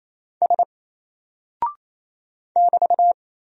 Morse code, heard as a keyed tone, about 27 wpm, 700 Hz. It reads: S <BT>
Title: Single Character (Letter:Number:Punctuation:Prosign) 28wpm